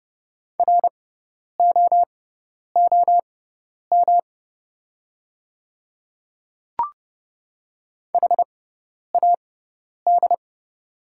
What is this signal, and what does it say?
Morse code, heard as a keyed tone, about 31 wpm, spaced as for 10 wpm, 700 Hz. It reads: ROOM HAD
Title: 6x Character Spacing - Top 300 Words - 30wpm